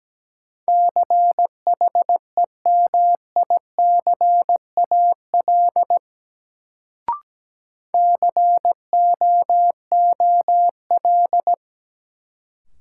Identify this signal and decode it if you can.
Morse code, heard as a keyed tone, about 17 wpm, 700 Hz. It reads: CHEMICAL COOL
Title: Sentences from Top 1200 Words - New Word Drill - 17wpm